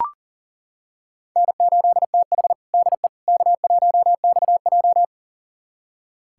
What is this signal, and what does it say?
Morse code, heard as a keyed tone, about 37 wpm, 700 Hz. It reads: N8TH DE K1XJ
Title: USA Callsign DE Callsign - 4 Characters - 40wpm